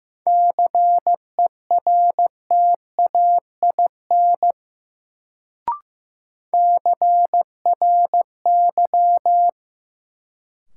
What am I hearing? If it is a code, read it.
Morse code, heard as a keyed tone, about 15 wpm, 700 Hz. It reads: CERTAIN CRY
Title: Sentences from Top 500 Words - New Word Drill - 15wpm